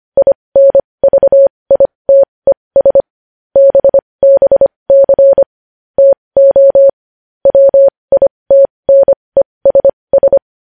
Morse code, heard as a keyed tone, about 25 wpm, spaced as for 20 wpm, 550 Hz. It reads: INVITES BBC TO WITNESS
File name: podcast / News headlines-1x-25x20